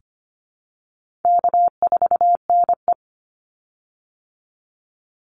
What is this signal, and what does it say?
Morse code, heard as a keyed tone, about 25 wpm, 700 Hz. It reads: K4NE
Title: Easy Call Signs Repeated 3x - 25wpm